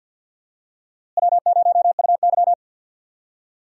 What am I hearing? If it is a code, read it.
Morse code, heard as a keyed tone, about 47 wpm, 700 Hz. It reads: W0UY
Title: USA Call Signs - 4 Characters - Repeated 3x - 50wpm